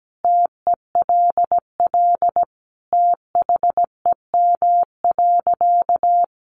Morse code, heard as a keyed tone, about 17 wpm, 700 Hz. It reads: TELL THEM.
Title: Sentences from Top 900 Words 17wpm